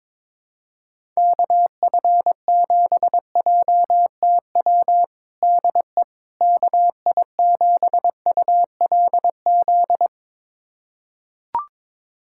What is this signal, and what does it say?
Morse code, heard as a keyed tone, about 22 wpm, 700 Hz. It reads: KF7JTW DE KI7ULZ
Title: USA Callsign DE Callsign 22wpm